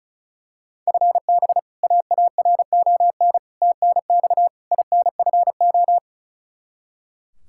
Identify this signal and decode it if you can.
Morse code, heard as a keyed tone, about 36 wpm, 700 Hz. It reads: FB AARON TNX INFO
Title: Practice QSO - Exchange 3 - More Info - 35wpm